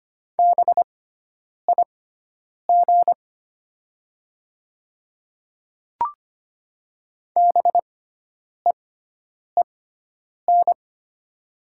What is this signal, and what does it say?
Morse code, heard as a keyed tone, about 25 wpm, spaced as for 9 wpm, 700 Hz. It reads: BIG BEEN
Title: 6x Character Spacing - Top 500 Words - 25wpm